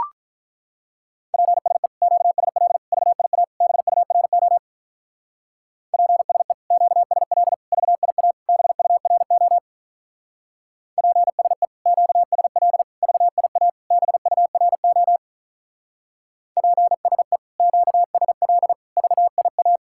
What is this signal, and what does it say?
Morse code, heard as a keyed tone, about 43 wpm, 700 Hz. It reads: PSE QSL VIA BURO PSE QSL VIA BURO PSE QSL VIA BURO PSE QSL VIA
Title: Speed-Racing Common QSO Phrases Encore - 35wpm